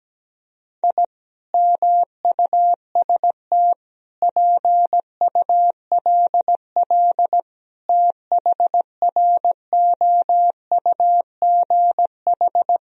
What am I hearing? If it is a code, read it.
Morse code, heard as a keyed tone, about 17 wpm, 700 Hz. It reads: I MUST PULL THROUGH
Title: Sentences from Top 1000 Words - Review Part 2 - 17wpm